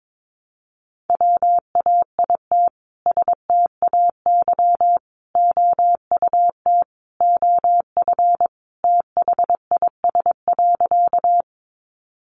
Morse code, heard as a keyed tone, about 22 wpm, 700 Hz. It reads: WAIT STAY OUT OF THIS.
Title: A Sound Of Thunder 22wpm